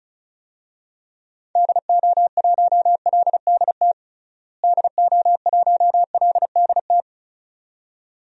Morse code, heard as a keyed tone, about 34 wpm, 700 Hz. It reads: DO1LDT DO1LDT
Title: DXCC-100 Call Signs - 6 Characters - Repeated 2x - 35wpm